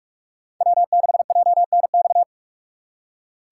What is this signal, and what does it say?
Morse code, heard as a keyed tone, about 45 wpm, 700 Hz. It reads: W6JNX